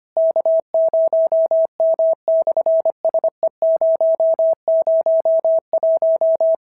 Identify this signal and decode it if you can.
Morse code, heard as a keyed tone, about 24 wpm, 650 Hz. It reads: K0M/SE001
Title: SOTA QSO Practice 25wpm